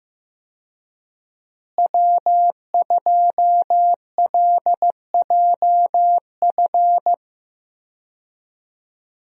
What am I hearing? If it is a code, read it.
Morse code, heard as a keyed tone, about 15 wpm, 700 Hz. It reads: W2LJF
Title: USA Call Signs - 5 Characters - Repeated 2x - 15wpm